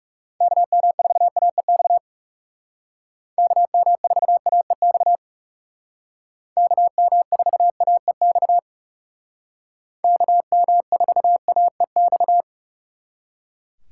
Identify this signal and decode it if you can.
Morse code, heard as a keyed tone, about 37 wpm, 700 Hz. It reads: KM4AEX KM4AEX KM4AEX KM4AEX
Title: Speed Racing - USA 2x3 Call Signs 30wpm